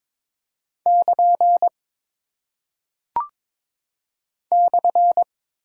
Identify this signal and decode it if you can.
Morse code, heard as a keyed tone, about 22 wpm, 700 Hz. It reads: <KN> /
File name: practice / 22wpm / Single Punctuation-Prosign - 22wpm